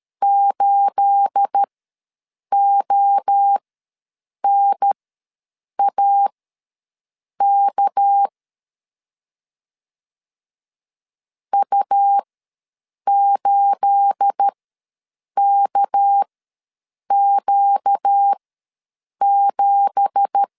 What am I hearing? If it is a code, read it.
Morse code, heard as a keyed tone, about 13 wpm, spaced as for 7 wpm, 800 Hz. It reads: 8ONAK U8KQ7
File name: LEZIONI / 9 2867 ws 12wpm 3431